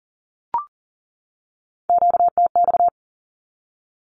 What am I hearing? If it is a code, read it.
Morse code, heard as a keyed tone, about 40 wpm, 700 Hz. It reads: QTX
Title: Q-Codes 40wpm